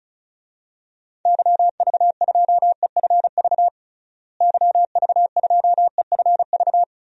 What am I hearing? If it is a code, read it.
Morse code, heard as a keyed tone, about 36 wpm, 700 Hz. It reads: YV2EFV YV2EFV
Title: DXCC-100 Call Signs - 6 Characters - Repeated 2x - 35wpm